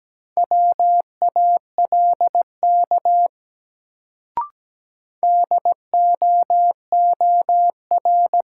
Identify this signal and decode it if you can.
Morse code, heard as a keyed tone, about 17 wpm, 700 Hz. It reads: WALK DOOR
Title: Sentences from Top 300 Words - New Word Drill - 17wpm